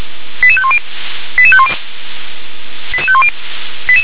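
Five-tone selective call (ZVEI), heard as quick runs of five tones, tones of about 70 ms. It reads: 88510 88510 88510
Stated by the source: ZVEI five-tone selective call 88607 received by SDR.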